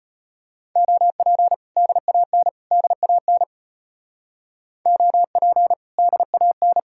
Morse code, heard as a keyed tone, about 32 wpm, 700 Hz. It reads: OP DAN DAN OP DAN